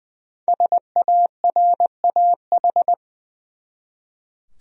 Morse code, heard as a keyed tone, about 20 wpm, 700 Hz. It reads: SARAH
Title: USA Common Names - Max 5 Letters - 20wpm